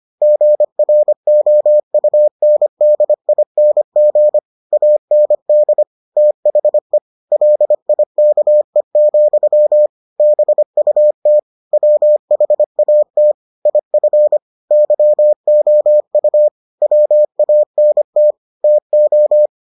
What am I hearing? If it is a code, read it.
Morse code, heard as a keyed tone, about 25 wpm, 600 Hz. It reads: GROUNDING AND THE LIKE, BUT WHAT IF YOU WANT TO